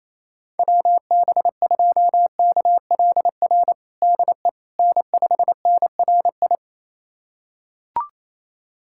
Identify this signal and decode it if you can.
Morse code, heard as a keyed tone, about 28 wpm, 700 Hz. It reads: WB2KLR DE N5NRI
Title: USA Callsign DE Callsign 28wpm